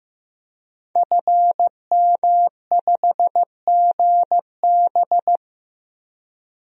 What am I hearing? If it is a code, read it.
Morse code, heard as a keyed tone, about 15 wpm, 700 Hz. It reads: FM5GB
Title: DXCC-100 Call Signs - 5 Characters - Repeated 3x - 15wpm